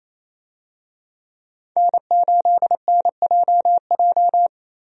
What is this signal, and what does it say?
Morse code, heard as a keyed tone, about 28 wpm, 700 Hz. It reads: N8NJJ
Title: USA Call Signs - 5 Characters - Repeated 2x - 28wpm